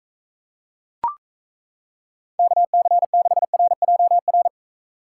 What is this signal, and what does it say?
Morse code, heard as a keyed tone, about 36 wpm, 700 Hz. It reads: KC6RJR